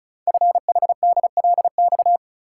Morse code, heard as a keyed tone, about 36 wpm, 700 Hz. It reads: FHDLX
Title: Code Groups - 5 Letter:Numbers - 35wpm